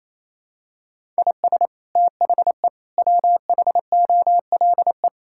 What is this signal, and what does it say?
Morse code, heard as a keyed tone, about 28 wpm, 700 Hz. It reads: IS THE WHOLE
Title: Sentences from Top 1000 Words - Review Part 5 - 28wpm